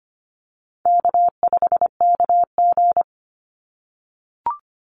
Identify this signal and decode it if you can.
Morse code, heard as a keyed tone, about 25 wpm, 700 Hz. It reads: K5KG
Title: Call Signs - 25wpm